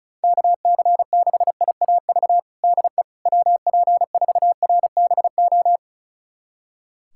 Morse code, heard as a keyed tone, about 36 wpm, 700 Hz. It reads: KC6IAV DE WP4RBO
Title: USA Callsign DE Callsign 35wpm